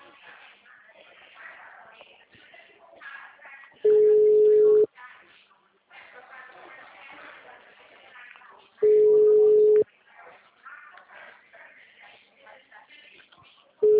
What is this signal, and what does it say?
Ringback tone (Europe, 425 Hz), with speech.
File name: out- -2012- - - .199